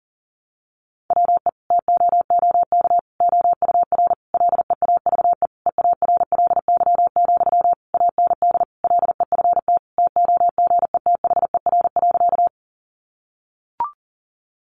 Morse code, heard as a keyed tone, about 40 wpm, 700 Hz. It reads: WE TOOK OUR LEAVE EARLY, AND LEFT TOGETHER.
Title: Sentences from Top 400 Words 40wpm